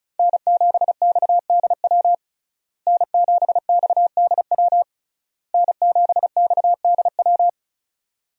Morse code, heard as a keyed tone, about 36 wpm, 700 Hz. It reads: N7XDW N7XDW N7XDW
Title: USA Call Signs - 5 Characters - Repeated 3x - 35wpm